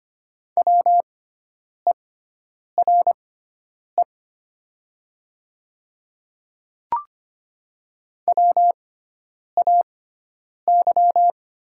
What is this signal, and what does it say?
Morse code, heard as a keyed tone, about 25 wpm, spaced as for 9 wpm, 700 Hz. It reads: WERE WAY
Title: 6x Character Spacing - Top 100 Words - 25wpm